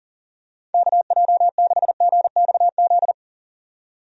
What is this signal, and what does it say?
Morse code, heard as a keyed tone, about 40 wpm, 700 Hz. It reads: KJ6GXZ
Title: USA Call Signs - 6 Characters - 40wpm